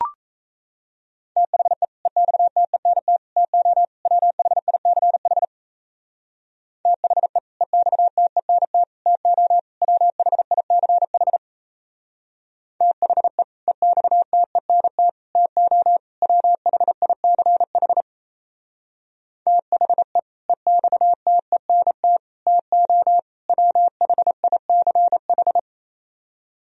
Morse code, heard as a keyed tone, about 35 wpm, 700 Hz. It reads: THE EXTENT TO WHICH THE EXTENT TO WHICH THE EXTENT TO WHICH THE EXTENT TO WHICH
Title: Speed-Racing - Sets of 4 Words 28wpm